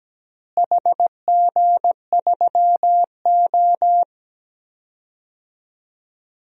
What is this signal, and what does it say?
Morse code, heard as a keyed tone, about 17 wpm, 700 Hz. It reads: HG3O
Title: DXCC-100 Call Signs - 4 Characters - Repeated 3x - 17wpm